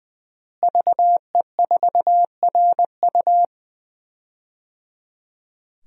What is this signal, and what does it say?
Morse code, heard as a keyed tone, about 20 wpm, 700 Hz. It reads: VE4RU
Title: DXCC-100 Call Signs - 5 Characters - Repeated 3x - 20wpm